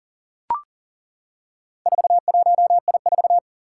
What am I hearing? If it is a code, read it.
Morse code, heard as a keyed tone, about 40 wpm, 700 Hz. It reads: 41I4
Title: Four Challenging Letter-Number Combinations 40wpm